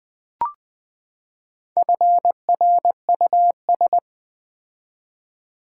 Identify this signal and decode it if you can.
Morse code, heard as a keyed tone, about 20 wpm, 700 Hz. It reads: FRUS
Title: Build A Word - 4 Letters At A Time - 11-Letter Words - 20wpm